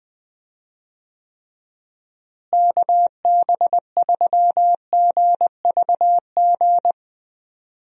Morse code, heard as a keyed tone, about 21 wpm, 700 Hz. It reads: KB3GVG
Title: USA Call Signs - 6 Characters - Repeated 3x - 20wpm